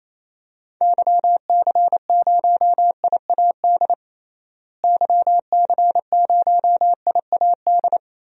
Morse code, heard as a keyed tone, about 28 wpm, 700 Hz. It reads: YC0IAD YC0IAD
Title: DXCC-100 Call Signs - 6 Characters - Repeated 2x - 28wpm